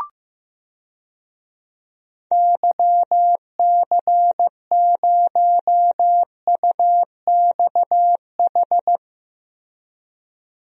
Morse code, heard as a keyed tone, about 15 wpm, 700 Hz. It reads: YC0UXH
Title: DXCC-100 Call Signs - 6 Characters - 15wpm